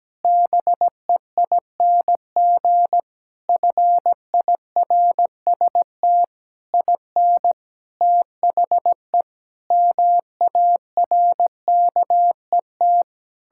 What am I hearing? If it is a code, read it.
Morse code, heard as a keyed tone, about 17 wpm, 700 Hz. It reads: BEING FIRST IN THE MARKET
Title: Sentences from Top 1600 Words 17wpm